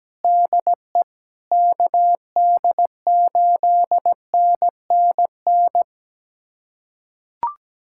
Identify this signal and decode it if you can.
Morse code, heard as a keyed tone, about 17 wpm, 700 Hz. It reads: DE KD8NNN
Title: USA Callsign DE Callsign - 6 Characters - 17wpm